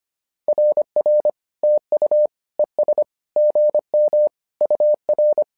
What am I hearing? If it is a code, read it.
Morse code, heard as a keyed tone, about 25 wpm, 600 Hz. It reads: RR TU ES GM UR
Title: SOTA QSO Practice 25wpm